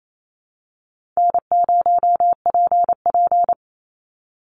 Morse code, heard as a keyed tone, about 28 wpm, 700 Hz. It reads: N0PP
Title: Easy Call Signs Repeated 2x - 28wpm